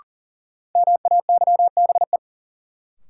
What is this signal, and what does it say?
Morse code, heard as a keyed tone, about 40 wpm, 700 Hz. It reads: MAYBE
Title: Sentences from Top 1000 Words - New Word Drill - 40wpm